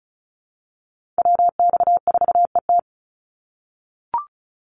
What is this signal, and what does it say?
Morse code, heard as a keyed tone, about 34 wpm, 700 Hz. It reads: WX4ET
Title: NA SOTA Chaser Callsigns 35wpm